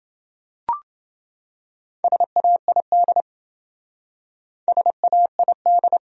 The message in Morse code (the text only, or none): SAID SAID